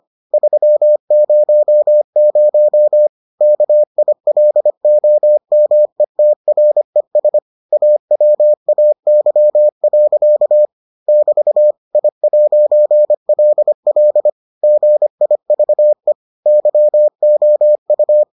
Morse code, heard as a keyed tone, about 25 wpm, 600 Hz. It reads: 300 KILOMETRES AWAY. <BT> I'LL GIVE YOU